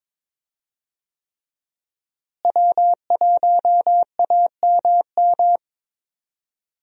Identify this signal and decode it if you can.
Morse code, heard as a keyed tone, about 22 wpm, 700 Hz. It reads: W1AMM
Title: USA Call Signs - 5 Characters - Repeated 3x - 22wpm